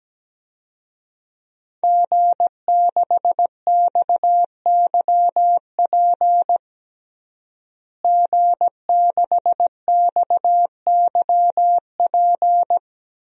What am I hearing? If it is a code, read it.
Morse code, heard as a keyed tone, about 17 wpm, 700 Hz. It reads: G6XYP G6XYP
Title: DXCC-100 Call Signs - 5 Characters - Repeated 3x - 17wpm